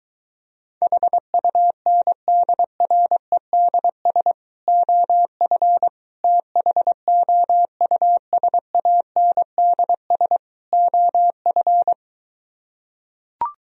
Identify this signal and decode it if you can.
Morse code, heard as a keyed tone, about 23 wpm, 700 Hz. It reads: HUNDREDS OF THOUSANDS OF